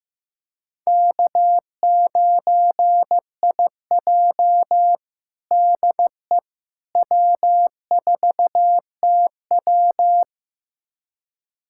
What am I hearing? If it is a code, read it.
Morse code, heard as a keyed tone, about 15 wpm, 700 Hz. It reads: K9IJ DE W4TW
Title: USA Callsign DE Callsign - 4 Characters - 15wpm